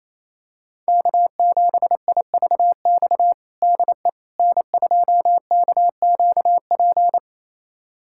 Morse code, heard as a keyed tone, about 28 wpm, 700 Hz. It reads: K7IVX DE N2KQP
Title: USA Callsign DE Callsign - 5 Characters - 28wpm